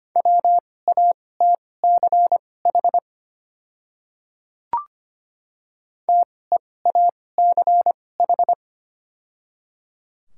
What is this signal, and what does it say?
Morse code, heard as a keyed tone, about 25 wpm, spaced as for 18 wpm, 700 Hz. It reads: WATCH TEACH
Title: 2x Character Spacing - Top 500 Words - 25wpm